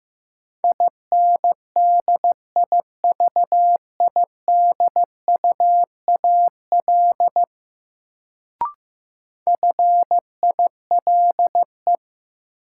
Morse code, heard as a keyed tone, about 16 wpm, 700 Hz. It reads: INDIVIDUAL FILE
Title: Sentences from Top 1000 Words - Review Part 4 - New Word Drill - 15wpm